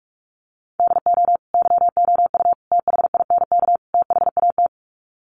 Morse code, heard as a keyed tone, about 45 wpm, 700 Hz. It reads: DO YOU THINK THAT
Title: Sets of 4 Words 45wpm